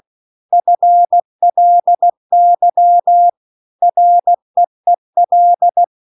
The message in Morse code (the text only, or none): FLY REEL